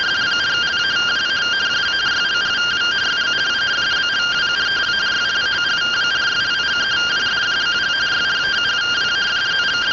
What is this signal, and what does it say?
Two-tone teleprinter signal, about 170 Hz shift.